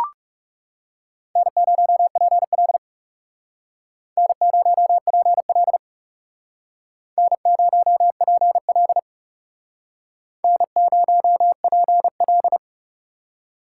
Morse code, heard as a keyed tone, about 38 wpm, 700 Hz. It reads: N0PL N0PL N0PL N0PL